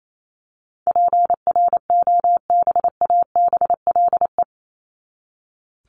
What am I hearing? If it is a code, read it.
Morse code, heard as a keyed tone, about 28 wpm, 700 Hz. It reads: PROBABLE
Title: LY Suffix - Top 100 Words with Base Words 28wpm